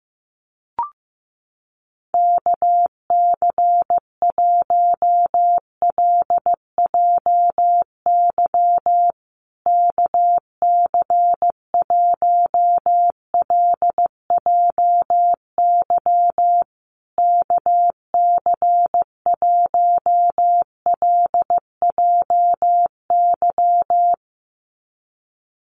Morse code, heard as a keyed tone, about 14 wpm, 700 Hz. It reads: KC1LJY KC1LJY KC1LJY
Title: USA Call Signs Repeated 3x 15wpm